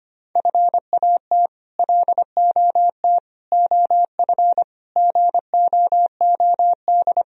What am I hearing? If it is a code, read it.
Morse code, heard as a keyed tone, about 25 wpm, 700 Hz. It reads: FAT LOT OF GOOD
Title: Sentences from Top 1000 Words - Review Part 4 - 25wpm